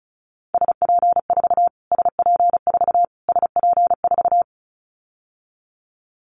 Morse code, heard as a keyed tone, about 34 wpm, 700 Hz. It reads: SP4 SP4 SP4
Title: Callsign Prefixes for Top 25 DXCC - Repeated 3x - 35wpm